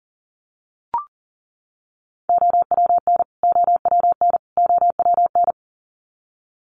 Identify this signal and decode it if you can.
Morse code, heard as a keyed tone, about 40 wpm, 700 Hz. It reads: OWN OWN OWN
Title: Top 500 Words Repeated 40wpm